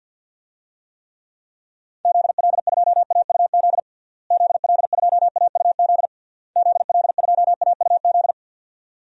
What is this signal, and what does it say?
Morse code, heard as a keyed tone, about 51 wpm, 700 Hz. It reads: ZL2AUZ ZL2AUZ ZL2AUZ
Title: DXCC-100 Call Signs - 6 Characters - Repeated 3x - 50wpm